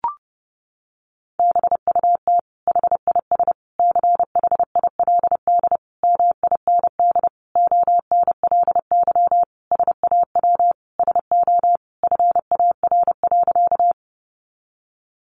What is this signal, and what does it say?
Morse code, heard as a keyed tone, about 30 wpm, 700 Hz. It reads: BUT HIS CHILD MIND ONLY SAW SO FAR.
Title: Sentences from Top 800 Words 30wpm